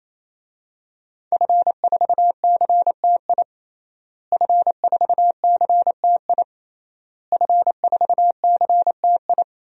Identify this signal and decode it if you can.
Morse code, heard as a keyed tone, about 28 wpm, 700 Hz. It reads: F4CTI F4CTI F4CTI
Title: DXCC-100 Call Signs - 5 Characters - Repeated 3x - 28wpm